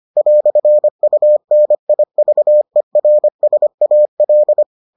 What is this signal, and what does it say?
Morse code, heard as a keyed tone, about 25 wpm, 600 Hz. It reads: "UNIVERSAL